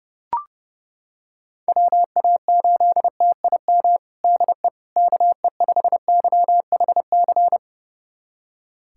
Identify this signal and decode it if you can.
Morse code, heard as a keyed tone, about 32 wpm, 700 Hz. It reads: WA8TIM DE KE5YHC
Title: USA Callsign DE Callsign 30wpm